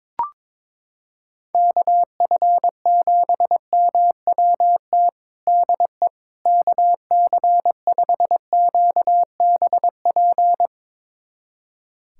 Morse code, heard as a keyed tone, about 22 wpm, 700 Hz. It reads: KF7MWT DE KC5QBP
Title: USA Callsign DE Callsign - 6 Characters - 22wpm